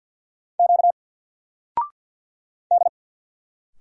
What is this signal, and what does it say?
Morse code, heard as a keyed tone, about 48 wpm, 700 Hz. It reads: Q D
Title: Single Letter-Number 50wpm